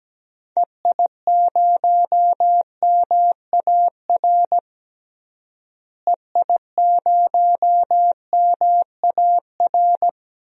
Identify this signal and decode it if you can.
Morse code, heard as a keyed tone, about 17 wpm, 700 Hz. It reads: EI0MAR EI0MAR
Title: DXCC-100 Call Signs - 6 Characters - Repeated 3x - 17wpm